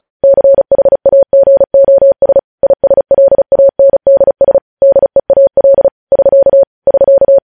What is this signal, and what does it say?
Morse code, heard as a keyed tone, about 36 wpm, 550 Hz. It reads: CHAGOS ISLANDS DEAL <SK> <SK>
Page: news headlines in morse code at 35 wpm